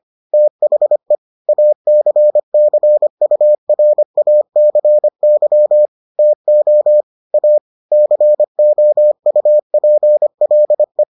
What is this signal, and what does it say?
Morse code, heard as a keyed tone, about 25 wpm, 600 Hz. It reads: THE ACCURACY TO A COUPLE